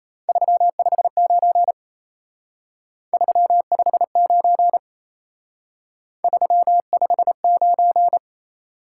Morse code, heard as a keyed tone, about 36 wpm, 700 Hz. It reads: 359 359 359